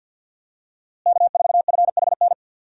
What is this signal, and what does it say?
Morse code, heard as a keyed tone, about 53 wpm, 700 Hz. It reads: K4FHN
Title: USA Call Signs - 5 Characters - Repeated 2x - 50wpm